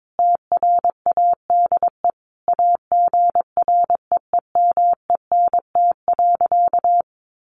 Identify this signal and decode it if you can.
Morse code, heard as a keyed tone, about 22 wpm, 700 Hz. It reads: TRADE AGREEMENT.
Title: Sentences from Top 1500 Words 22wpm